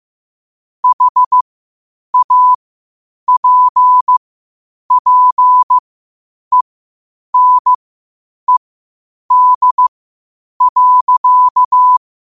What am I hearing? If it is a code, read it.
Morse code, heard as a keyed tone, about 15 wpm, spaced as for 8 wpm, 1000 Hz. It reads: HAPPENED.
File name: qotdfiles / 8wpm